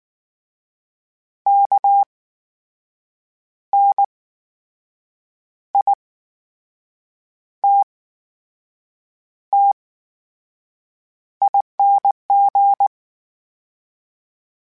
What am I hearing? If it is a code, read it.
Morse code, heard as a keyed tone, about 19 wpm, spaced as for 6 wpm, 800 Hz. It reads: KNITTING
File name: Special INGs / Special INGs-mp3s / VerySlowToFaster Ing @18